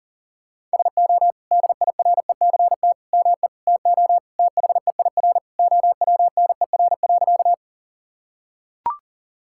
Morse code, heard as a keyed tone, about 40 wpm, 700 Hz. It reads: SO DIRECT ME TO THEIR OWNER.
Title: Sentences from Top 1600 Words 40wpm